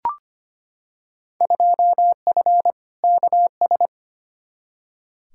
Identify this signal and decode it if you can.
Morse code, heard as a keyed tone, about 25 wpm, 700 Hz. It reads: 2F KS
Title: Field Day Exchanges Encore - 25wpm